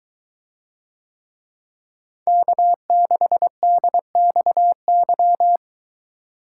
Morse code, heard as a keyed tone, about 22 wpm, 700 Hz. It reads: K6DXY